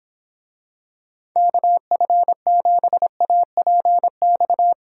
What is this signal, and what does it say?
Morse code, heard as a keyed tone, about 26 wpm, 700 Hz. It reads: KF7APX